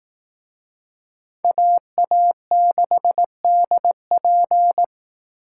Morse code, heard as a keyed tone, about 18 wpm, 700 Hz. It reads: AA6DP
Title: Speed Racing - USA 2x2 Call Signs 15wpm